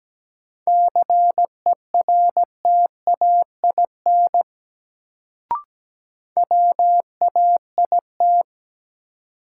Morse code, heard as a keyed tone, about 17 wpm, 700 Hz. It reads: CERTAIN WAIT
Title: Sentences from Top 500 Words - New Word Drill - 17wpm